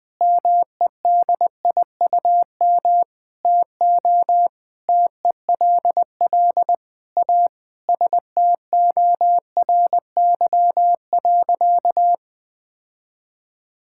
Morse code, heard as a keyed tone, about 20 wpm, 700 Hz. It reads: MEDIUM TO TELL A STORY.
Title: Sentences from Top 1000 Words - Review Part 6 - 20wpm